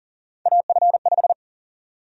Morse code, heard as a keyed tone, about 40 wpm, 700 Hz. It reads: AF5
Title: USA Call Sign Prefixes - Repeated 2x - 40wpm